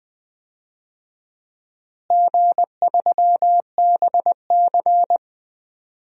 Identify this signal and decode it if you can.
Morse code, heard as a keyed tone, about 20 wpm, 700 Hz. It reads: G3BC
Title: DXCC-100 Call Signs - 4 Characters - Repeated 3x - 20wpm